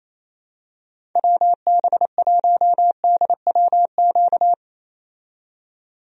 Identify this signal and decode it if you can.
Morse code, heard as a keyed tone, about 28 wpm, 700 Hz. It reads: WB1DWQ
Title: USA Call Signs - 6 Characters - 28wpm